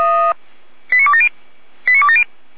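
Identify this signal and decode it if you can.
Five-tone selective call (ZVEI), heard as quick runs of five tones, tones of about 70 ms. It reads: 89270 89270
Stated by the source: ZVEI five-tone call received by SDR, from the audio server of Feuerwehr Orlamünde (f85360).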